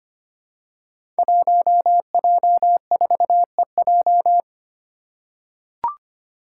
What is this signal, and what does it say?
Morse code, heard as a keyed tone, about 25 wpm, 700 Hz. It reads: 1J4EJ
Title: Five Challenging Letter-Number Combinations 25wpm